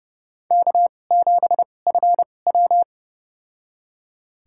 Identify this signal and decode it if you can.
Morse code, heard as a keyed tone, about 32 wpm, spaced as for 22 wpm, 700 Hz. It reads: K7FW
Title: USA Call Signs - 4 Characters - 2x Character Spacing - 30wpm